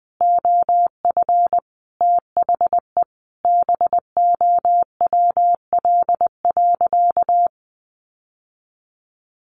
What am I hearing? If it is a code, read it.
Morse code, heard as a keyed tone, about 20 wpm, 700 Hz. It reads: OF THE BOWL.
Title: Sentences from Top 1200 Words 20wpm